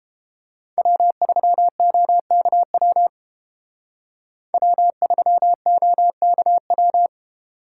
Morse code, heard as a keyed tone, about 32 wpm, 700 Hz. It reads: W3OKW W3OKW